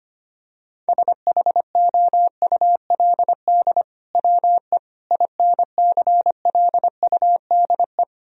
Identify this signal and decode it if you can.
Morse code, heard as a keyed tone, about 25 wpm, 700 Hz. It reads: SHOULD WE INCLUDE
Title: Sentences from Top 1000 Words - Review Part 3 - 25wpm